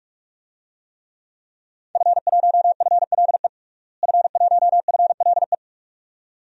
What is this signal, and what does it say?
Morse code, heard as a keyed tone, about 46 wpm, 700 Hz. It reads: F1FLE F1FLE
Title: DXCC-100 Call Signs - 5 Characters - Repeated 2x - 45wpm